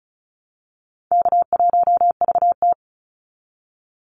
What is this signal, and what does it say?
Morse code, heard as a keyed tone, about 34 wpm, 700 Hz. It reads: K1VT
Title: Easy Call Signs Repeated 3x - 35wpm